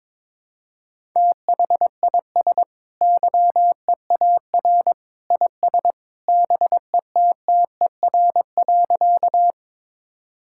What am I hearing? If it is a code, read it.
Morse code, heard as a keyed tone, about 22 wpm, 700 Hz. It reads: THIS YEAR IS BETTER.
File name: practice / 22wpm / Sentences with 2-Word Phrases - 22wpm